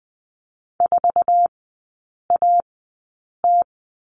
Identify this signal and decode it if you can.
Morse code, heard as a keyed tone, about 20 wpm, spaced as for 8 wpm, 700 Hz. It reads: 4AT
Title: 067 Lesson 12e - Letter d - Three Characters 20wpm